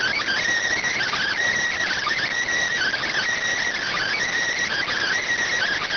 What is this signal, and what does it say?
Emergency Alert System (SAME) tones (header bursts), heard as harsh buzzing bursts. It is heard from 0.8 to 5.2 s.